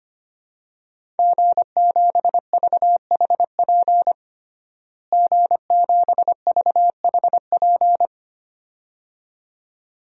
Morse code, heard as a keyed tone, about 25 wpm, 700 Hz. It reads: G7VHP G7VHP
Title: DXCC-100 Call Signs - 5 Characters - Repeated 2x - 25wpm